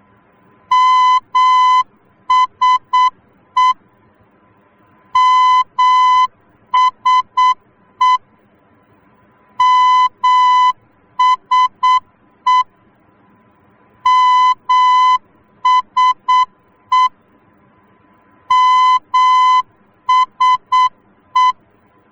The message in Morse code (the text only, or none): MSE MSE MSE MSE MSE